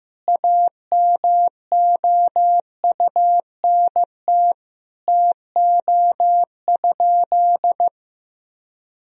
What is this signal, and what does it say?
Morse code, heard as a keyed tone, about 15 wpm, 700 Hz. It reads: AMOUNT TO?
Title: Sentences from Top 1000 Words - Review Part 4 - 15wpm